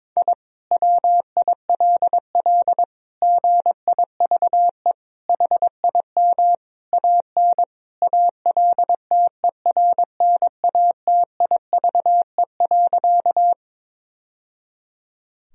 Morse code, heard as a keyed tone, about 22 wpm, 700 Hz. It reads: I WILL GIVE HIM AN ALTERNATIVE.
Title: Sentences from Top 1000 Words - Review Part 4 - 22wpm